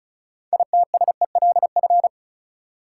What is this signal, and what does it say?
Morse code, heard as a keyed tone, about 34 wpm, 700 Hz. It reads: ITSELF
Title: Sentences from Top 800 Words - New Word Drill - 35wpm